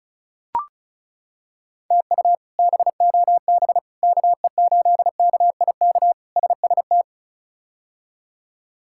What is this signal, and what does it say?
Morse code, heard as a keyed tone, about 34 wpm, 700 Hz. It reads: TU BOB KE8KIK SST
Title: K1USN SST QSO Practice - 35wpm